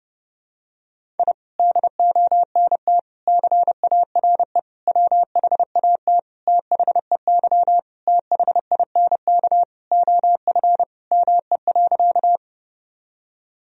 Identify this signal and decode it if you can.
Morse code, heard as a keyed tone, about 32 wpm, 700 Hz. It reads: I DONT CARE WHAT THEY THINK OF ME.
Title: Sentences from Top 1000 Words - Review Part 1 - 30wpm